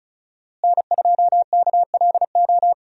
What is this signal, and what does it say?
Morse code, heard as a keyed tone, about 34 wpm, 700 Hz. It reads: N2KLO
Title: USA Call Signs - 5 Characters - Repeated 3x - 35wpm